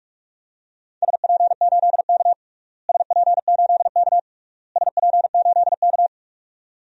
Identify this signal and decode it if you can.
Morse code, heard as a keyed tone, about 41 wpm, 700 Hz. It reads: SP8K SP8K SP8K
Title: DXCC-100 Call Signs - 4 Characters - Repeated 3x - 45wpm